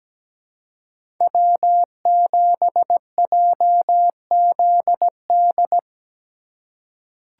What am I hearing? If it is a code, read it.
Morse code, heard as a keyed tone, about 17 wpm, 700 Hz. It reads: W7JZD
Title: USA Call Signs - 5 Characters - 3x Character Spacing - 17wpm